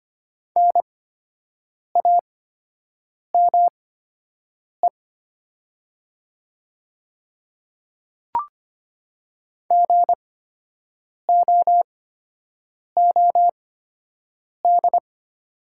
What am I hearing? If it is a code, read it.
Morse code, heard as a keyed tone, about 25 wpm, spaced as for 7 wpm, 700 Hz. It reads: NAME GOOD
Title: 8x Character Spacing - Top 200 Words - 25wpm